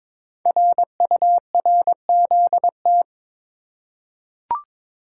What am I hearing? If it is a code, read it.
Morse code, heard as a keyed tone, about 22 wpm, 700 Hz. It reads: RURZT
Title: Code Groups - 5 Letters - 22wpm